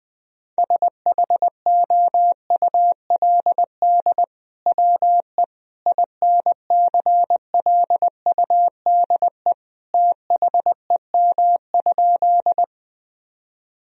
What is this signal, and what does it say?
Morse code, heard as a keyed tone, about 20 wpm, 700 Hz. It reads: SHOULD WE INCLUDE THEM?
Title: Sentences from Top 1000 Words - Review Part 3 - 20wpm